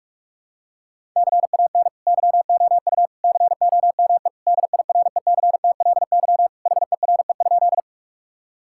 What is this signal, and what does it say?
Morse code, heard as a keyed tone, about 46 wpm, 700 Hz. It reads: CAN YOU COME DIRECTLY HERE?
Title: Sentences from Top 1000 Words - Review Part 5 - 45wpm